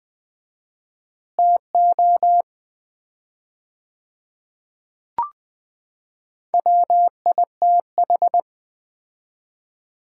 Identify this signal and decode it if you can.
Morse code, heard as a keyed tone, about 20 wpm, 700 Hz. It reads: TO WITH
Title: 4x Word-Spacing - Sets of 4 Words - 20wpm